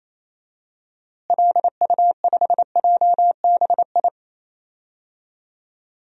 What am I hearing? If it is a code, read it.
Morse code, heard as a keyed tone, about 27 wpm, 700 Hz. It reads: LU5JBI
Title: DXCC-100 Call Signs - 6 Characters - Repeated 3x - 28wpm